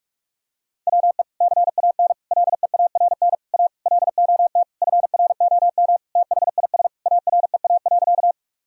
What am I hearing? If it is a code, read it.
Morse code, heard as a keyed tone, about 45 wpm, 700 Hz. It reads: WE CAN LEARN A LOT FROM THIS AREA.
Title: Sentences from Top 1000 Words - Review Part 2 - 45wpm